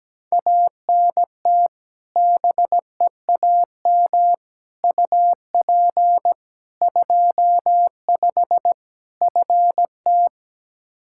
Morse code, heard as a keyed tone, about 17 wpm, 700 Hz. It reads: ANT BEAM UP 25 FT
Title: K1USN SST QSO Practice 17wpm